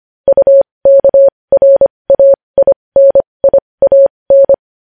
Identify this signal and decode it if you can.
Morse code, heard as a keyed tone, about 25 wpm, spaced as for 20 wpm, 550 Hz. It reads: UKRAINIAN
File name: podcast / News headlines-1x-25x20